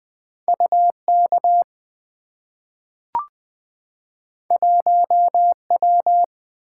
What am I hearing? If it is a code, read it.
Morse code, heard as a keyed tone, about 20 wpm, 700 Hz. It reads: U K 1 W
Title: Two Letter-Number Combinations - 20wpm